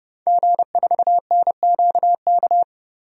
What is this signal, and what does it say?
Morse code, heard as a keyed tone, about 30 wpm, 700 Hz. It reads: G4NQK
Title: DXCC-100 Call Signs - 5 Characters - 30wpm